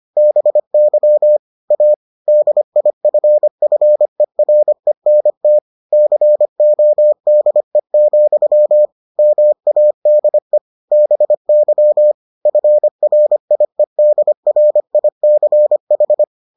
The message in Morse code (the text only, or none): BY A DIFFERENT CODE, MADE BY FRIEDRICH